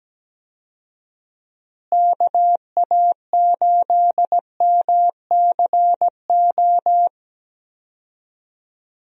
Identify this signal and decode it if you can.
Morse code, heard as a keyed tone, about 17 wpm, 700 Hz. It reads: KA8MCO
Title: USA Call Signs - 6 Characters - Repeated 3x - 17wpm